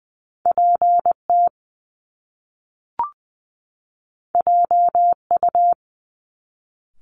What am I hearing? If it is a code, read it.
Morse code, heard as a keyed tone, about 20 wpm, 700 Hz. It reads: PT JU
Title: Top Two Letter Combinations in Words 20wpm